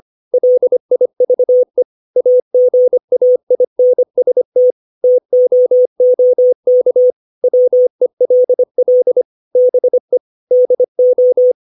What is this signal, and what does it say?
Morse code, heard as a keyed tone, about 25 wpm, 500 Hz. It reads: LIVE AGAINST TOOK WELL BE DO
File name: 632 words / 632 Words 25WPM